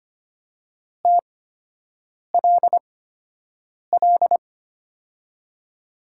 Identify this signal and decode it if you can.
Morse code, heard as a keyed tone, about 25 wpm, spaced as for 7 wpm, 700 Hz. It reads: TLL
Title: USA Call Signs - 6 Characters - 8x Character Spacing - 25wpm